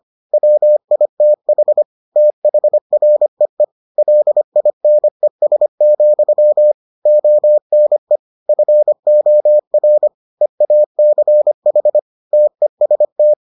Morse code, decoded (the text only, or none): WITH THREE LINES, ONE FOR EACH TEST